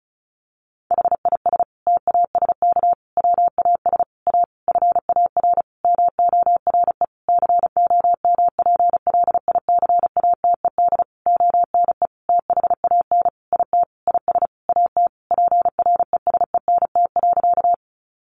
Morse code, heard as a keyed tone, about 34 wpm, 700 Hz. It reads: HIS TASK WAS A FAR MORE COMPLICATED ONE THAN IT IS AT PRESENT.
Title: Sentences from Top 1200 Words 35wpm